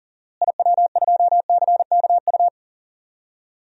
Morse code, heard as a keyed tone, about 40 wpm, 700 Hz. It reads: IW2CKU
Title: DXCC-100 Call Signs - 6 Characters - Repeated 2x - 40wpm